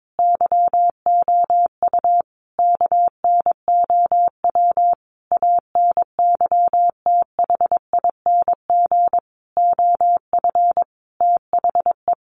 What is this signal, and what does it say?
Morse code, heard as a keyed tone, about 22 wpm, 700 Hz. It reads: YOU KNOW ANYTHING OF THE
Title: Sentences from Top 1000 Words 22wpm